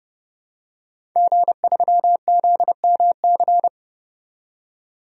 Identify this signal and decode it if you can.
Morse code, heard as a keyed tone, about 30 wpm, 700 Hz. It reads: G3ZMC
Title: DXCC-100 Call Signs - 5 Characters - 30wpm